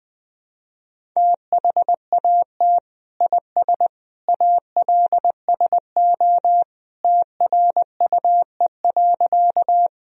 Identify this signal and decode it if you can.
Morse code, heard as a keyed tone, about 21 wpm, 700 Hz. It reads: THAT IS ALSO TRUE.
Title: Sentences from Top 1000 Words - Review Part 1 - 20wpm